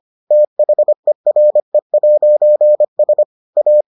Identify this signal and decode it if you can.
Morse code, heard as a keyed tone, about 25 wpm, 600 Hz. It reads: THERE'S A